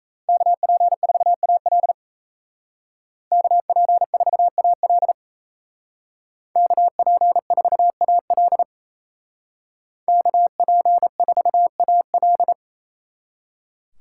Morse code, heard as a keyed tone, about 36 wpm, 700 Hz. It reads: KP4AL KP4AL KP4AL KP4AL